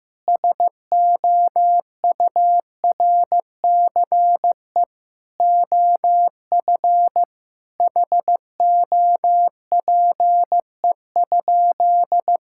Morse code, heard as a keyed tone, about 15 wpm, 700 Hz. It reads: SOURCE OF HOPE?
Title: Sentences from Top 1000 Words - Review Part 5 - 15wpm